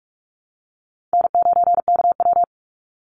Morse code, heard as a keyed tone, about 44 wpm, 700 Hz. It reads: N9KW
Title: NA SOTA Chaser Callsigns 45wpm